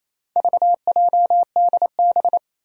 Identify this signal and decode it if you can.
Morse code, heard as a keyed tone, about 28 wpm, 700 Hz. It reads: VJDB
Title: Four Challenging Letter-Number Combinations 28wpm